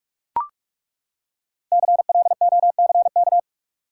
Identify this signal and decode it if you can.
Morse code, heard as a keyed tone, about 42 wpm, 700 Hz. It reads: CLOCK
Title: Five Letter Words - 45wpm